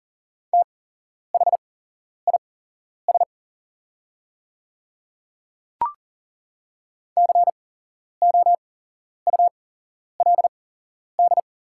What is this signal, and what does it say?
Morse code, heard as a keyed tone, about 40 wpm, spaced as for 11 wpm, 700 Hz. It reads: THIS COULD
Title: 8x Character Spacing - Top 100 Words - 40wpm